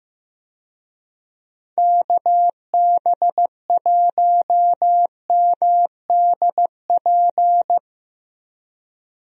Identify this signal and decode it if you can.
Morse code, heard as a keyed tone, about 15 wpm, 700 Hz. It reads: KB1MDP
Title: USA Call Signs - 6 Characters - Repeated 2x - 15wpm